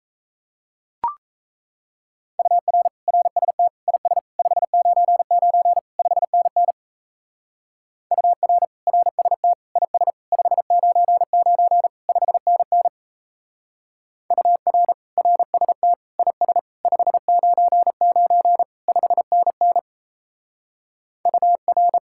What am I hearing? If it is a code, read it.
Morse code, heard as a keyed tone, about 32 wpm, 700 Hz. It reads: UR RST IS 599 5NN UR RST IS 599 5NN UR RST IS 599 5NN UR
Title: Speed-Racing Common QSO Phrases - 28wpm